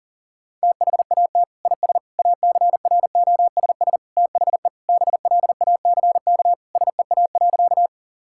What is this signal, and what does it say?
Morse code, heard as a keyed tone, about 40 wpm, 700 Hz. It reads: THAT IS ACROSS THE BLACK SEA.
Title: Sentences from Top 1000 Words - Review Part 5 - 40wpm